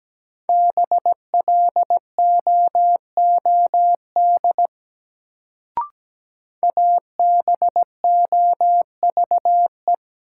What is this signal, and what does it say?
Morse code, heard as a keyed tone, about 17 wpm, 700 Hz. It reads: BLOOD ABOVE
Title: Sentences from Top 1000 Words - Review Part 2 - New Word Drill - 17wpm